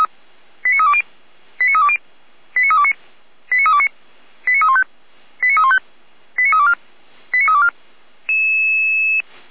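Five-tone selective call (ZVEI), heard as quick runs of five tones, tones of about 70 ms. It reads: 89320 89320 89329 89329 89315 89315 89324 89324
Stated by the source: ZVEI five-tone selective call received with an SDR, from a fire brigade.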